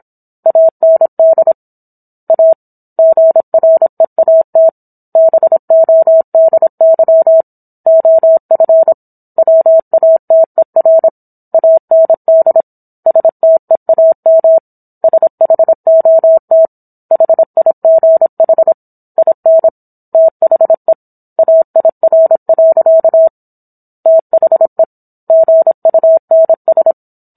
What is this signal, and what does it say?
Morse code, heard as a keyed tone, about 26 wpm, 650 Hz. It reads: AND A GREAT BODY OF WATER AND STEAM SHOT HIGH IN THE AIR. THE GUNS